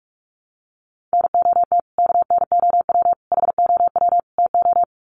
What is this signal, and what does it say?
Morse code, heard as a keyed tone, about 44 wpm, 700 Hz. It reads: NOT KNOW HOW TO
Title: Sets of 4 Words 45wpm